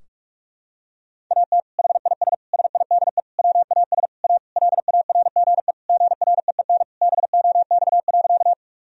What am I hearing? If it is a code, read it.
Morse code, heard as a keyed tone, about 46 wpm, 700 Hz. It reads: AT HIS SIDE WAS A LARGE GREEN BOX.
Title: Sentences from Top 1000 Words - Review Part 2 - 45wpm